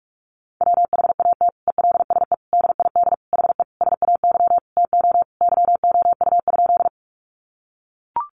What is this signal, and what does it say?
Morse code, heard as a keyed tone, about 44 wpm, 700 Hz. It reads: WHAT ELSE DID HE SAY TO YOU?
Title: Sentences from Top 800 Words 45wpm